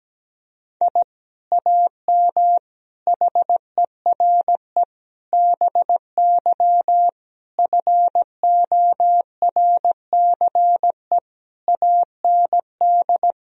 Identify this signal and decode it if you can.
Morse code, heard as a keyed tone, about 17 wpm, 700 Hz. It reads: I AM HERE BY FORCE AND
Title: Sentences from Top 1000 Words - Review Part 3 - 17wpm